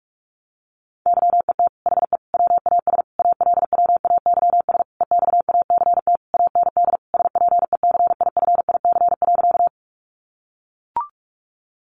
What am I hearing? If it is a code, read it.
Morse code, heard as a keyed tone, about 44 wpm, 700 Hz. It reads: YET HE WAS ALWAYS EXACT AND SPECIFIC.
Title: Sentences from Top 1300 Words 45wpm